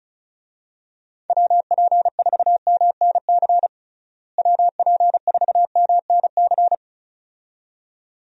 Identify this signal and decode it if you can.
Morse code, heard as a keyed tone, about 34 wpm, 700 Hz. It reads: WP4MNC WP4MNC
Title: USA Call Signs - 6 Characters - Repeated 2x - 35wpm